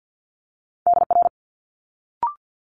Morse code, heard as a keyed tone, about 50 wpm, 700 Hz. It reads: DR
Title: Multi-letter Phonemes Quick Fire 50wpm